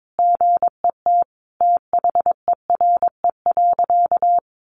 Morse code, heard as a keyed tone, about 22 wpm, 700 Hz. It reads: GET THERE.
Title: Sentences from Top 1500 Words 22wpm